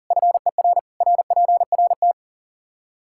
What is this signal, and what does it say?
Morse code, heard as a keyed tone, about 40 wpm, 700 Hz. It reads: FER RPRT
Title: Practice QSO - Exchange 2 - Info - 40wpm